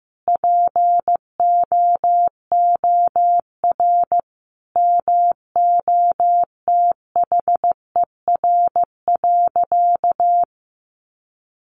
Morse code, heard as a keyed tone, about 15 wpm, 700 Hz. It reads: POOR MOTHER.
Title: Sentences from Top 1000 Words 15wpm